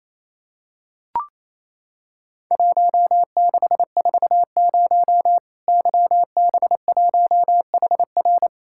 Morse code, heard as a keyed tone, about 28 wpm, 700 Hz. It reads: 1640 YB1HR
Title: Worked All Europe DX Contest - QTC Exchange - 28wpm